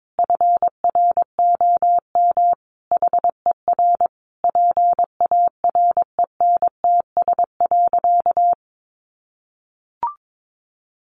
Morse code, heard as a keyed tone, about 22 wpm, 700 Hz. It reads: FROM HER PARENTS.
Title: Easy Phrases 22wpm